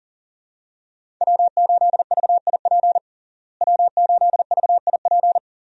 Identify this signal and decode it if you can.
Morse code, heard as a keyed tone, about 37 wpm, 700 Hz. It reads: W8VIP W8VIP
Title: USA Call Signs - 5 Characters - Repeated 2x - 40wpm